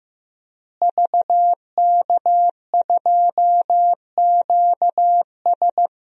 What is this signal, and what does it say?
Morse code, heard as a keyed tone, about 15 wpm, 700 Hz. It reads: VK2QS
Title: DXCC-100 Call Signs - 5 Characters - Repeated 2x - 15wpm